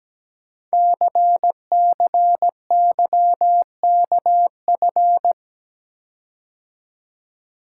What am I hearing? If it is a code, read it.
Morse code, heard as a keyed tone, about 17 wpm, 700 Hz. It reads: CCYKF
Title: Code Groups - 5 Letters - 17wpm